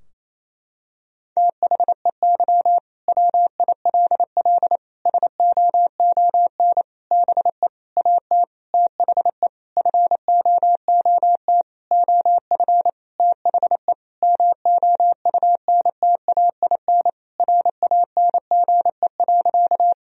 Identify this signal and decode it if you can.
Morse code, heard as a keyed tone, about 28 wpm, 700 Hz. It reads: THEY WILL SOON BE AT THE FOOT OF THE MOUNTAIN RANGE.
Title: Sentences from Top 1000 Words - Review Part 4 - 28wpm